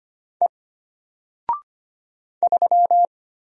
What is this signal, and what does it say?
Morse code, heard as a keyed tone, about 25 wpm, 700 Hz. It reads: E 3
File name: Single-Letter-Number-25wpm